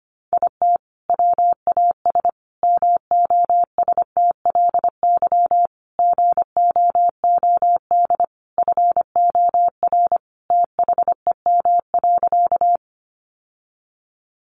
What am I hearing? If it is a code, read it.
Morse code, heard as a keyed tone, about 25 wpm, 700 Hz. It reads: IT WAS MOSTLY GOOD FOR THEM.
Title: Sentences from Top 1300 Words 25wpm